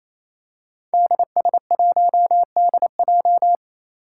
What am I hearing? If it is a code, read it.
Morse code, heard as a keyed tone, about 28 wpm, 700 Hz. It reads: DS1DJ
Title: Five Challenging Letter-Number Combinations 28wpm